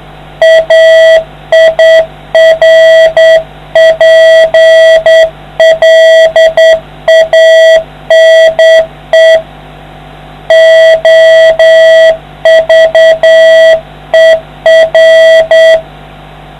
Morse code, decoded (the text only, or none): AIRPLANE OVER